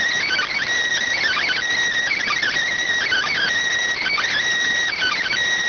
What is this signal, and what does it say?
Emergency Alert System (SAME) tones, heard as harsh buzzing bursts. It is heard from 1.2 to 5.3 s.